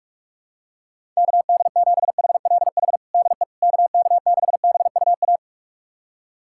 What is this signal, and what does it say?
Morse code, heard as a keyed tone, about 45 wpm, 700 Hz. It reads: KD7HLH DE KK6BUA
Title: USA Callsign DE Callsign - 6 Characters - 45wpm